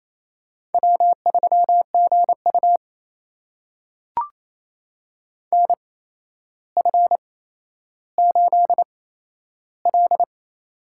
Morse code, heard as a keyed tone, about 28 wpm, spaced as for 12 wpm, 700 Hz. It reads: W3GU NF8L
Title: USA Call Signs - 4 Characters - 8x Character Spacing - 28wpm